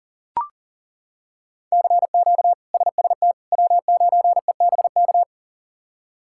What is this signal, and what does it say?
Morse code, heard as a keyed tone, about 40 wpm, 700 Hz. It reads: CQ SST W9EBK
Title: K1USN SST QSO Practice - 40wpm